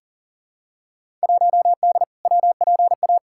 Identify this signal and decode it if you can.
Morse code, heard as a keyed tone, about 40 wpm, 700 Hz. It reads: 1D WPA
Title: Field Day Exchanges - 40wpm